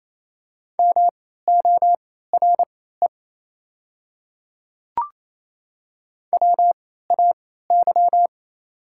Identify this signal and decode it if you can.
Morse code, heard as a keyed tone, about 28 wpm, spaced as for 16 wpm, 700 Hz. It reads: MORE WAY
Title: 3x Character Spacing - Top 200 Words - 28wpm